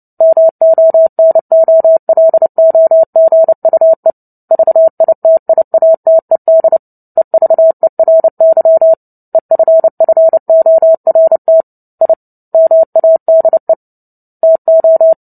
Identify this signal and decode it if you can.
Morse code, heard as a keyed tone, about 30 wpm, 650 Hz. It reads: MONOLOGUE VITIATED EVERY EFFORT I MADE TO
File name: Book Two / 03-Chapter 3 - 29 WPM